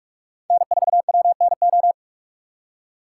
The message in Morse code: N4WNO

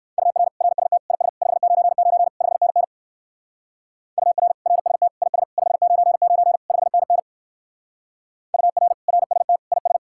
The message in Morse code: UR RST IS 599 5NN UR RST IS 599 5NN UR RST IS

